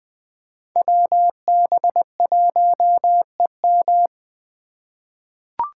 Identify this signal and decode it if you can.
Morse code, heard as a keyed tone, about 20 wpm, 700 Hz. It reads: WB1EM
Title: Code Groups - 5 Letter:Numbers - 20wpm